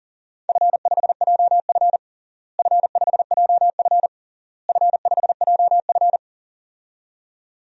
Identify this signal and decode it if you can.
Morse code, heard as a keyed tone, about 40 wpm, 700 Hz. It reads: F5JF F5JF F5JF
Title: DXCC-100 Call Signs - 4 Characters - Repeated 3x - 40wpm